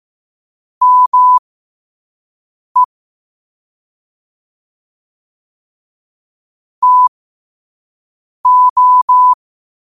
Morse code, heard as a keyed tone, about 15 wpm, spaced as for 5 wpm, 1000 Hz. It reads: ME TO
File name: qotdfiles / 5wpm